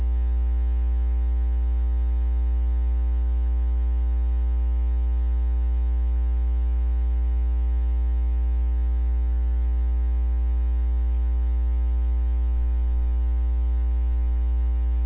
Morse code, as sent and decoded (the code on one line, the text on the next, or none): none